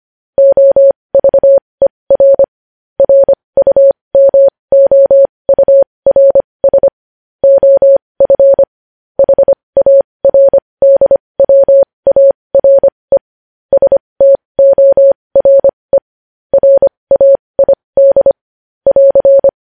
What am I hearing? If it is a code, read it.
Morse code, heard as a keyed tone, about 25 wpm, spaced as for 20 wpm, 550 Hz. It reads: OVER RUMOURS OF HARDWARE STORE RAID <AR>
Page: news-headlines-in-morse-code-at-20-wpm